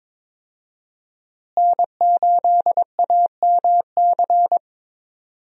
Morse code, heard as a keyed tone, about 22 wpm, 700 Hz. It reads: N8AMC
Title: USA Call Signs - 5 Characters - Repeated 3x - 22wpm